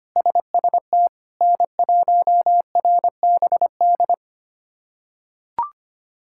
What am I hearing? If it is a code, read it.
Morse code, heard as a keyed tone, about 25 wpm, 700 Hz. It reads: SST N1RBD
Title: K1USN SST QSO Practice - 25wpm